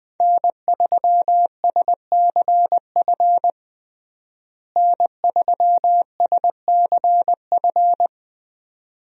Morse code, heard as a keyed tone, about 20 wpm, 700 Hz. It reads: N3SCF N3SCF
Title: USA Call Signs - 5 Characters - Repeated 2x - 20wpm